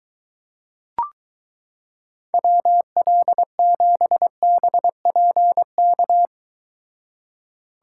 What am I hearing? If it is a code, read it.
Morse code, heard as a keyed tone, about 23 wpm, 700 Hz. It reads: WL7BPK